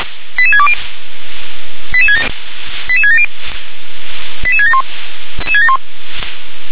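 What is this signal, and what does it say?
Five-tone selective call (ZVEI), heard as quick runs of five tones, tones of about 70 ms. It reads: 88620 88670 88671
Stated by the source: ZVEI five-tone selective call received by SDR.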